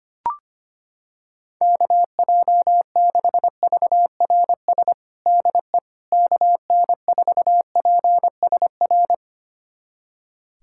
Morse code, heard as a keyed tone, about 25 wpm, 700 Hz. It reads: KJ6VRS DE KN4PSR
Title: USA Callsign DE Callsign - 6 Characters - 25wpm